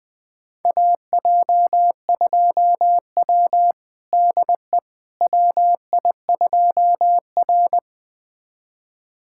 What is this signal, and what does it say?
Morse code, heard as a keyed tone, about 20 wpm, 700 Hz. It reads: AJ2W DE WI2R
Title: USA Callsign DE Callsign - 4 Characters - 20wpm